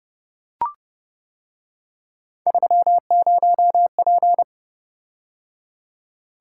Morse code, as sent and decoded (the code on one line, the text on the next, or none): ...-- ----- .--.
30P